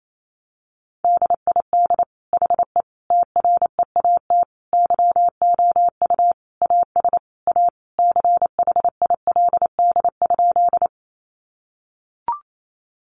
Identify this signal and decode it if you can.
Morse code, heard as a keyed tone, about 28 wpm, 700 Hz. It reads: DID HE TREAT YOU AS A CHILD?
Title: Sentences from Top 1300 Words 28wpm